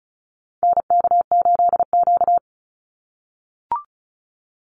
Morse code, heard as a keyed tone, about 34 wpm, 700 Hz. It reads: NK8Q
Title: NA SOTA Activator Callsigns 35wpm